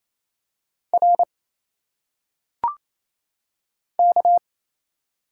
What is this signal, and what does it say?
Morse code, heard as a keyed tone, about 27 wpm, 700 Hz. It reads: R K
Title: Single Letters - Phonetic Pronunciation 28wpm